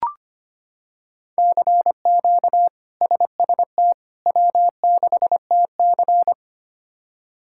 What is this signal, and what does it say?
Morse code, heard as a keyed tone, about 25 wpm, 700 Hz. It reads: CQ SST W6TC
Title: K1USN SST QSO Practice - 25wpm